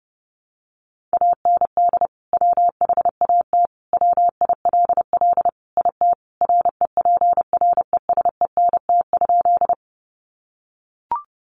Morse code, heard as a keyed tone, about 30 wpm, 700 Hz. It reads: AND WHAT WILL IT REPRESENT?
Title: Sentences from Top 1200 Words 30wpm